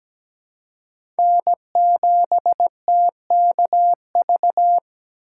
Morse code, heard as a keyed tone, about 17 wpm, 700 Hz. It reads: N7TKV
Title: USA Call Signs - 4x Character Spacing - 17wpm